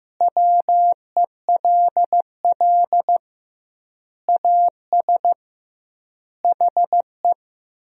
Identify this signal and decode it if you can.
Morse code, heard as a keyed tone, about 16 wpm, 700 Hz. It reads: WELL AS HE
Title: 2x Word-Spacing - Sentences from Top 500 Words - 15wpm